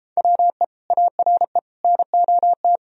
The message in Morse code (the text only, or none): WE ARE NOT